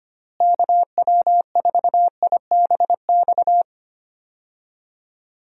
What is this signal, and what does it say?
Morse code, heard as a keyed tone, about 25 wpm, 700 Hz. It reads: KW4IBX
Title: USA Call Signs - 6 Characters - Repeated 3x - 25wpm